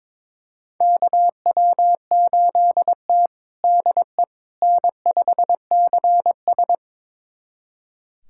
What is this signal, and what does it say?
Morse code, heard as a keyed tone, about 22 wpm, 700 Hz. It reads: KW8T DE N5CS
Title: USA Callsign DE Callsign - 4 Characters - 22wpm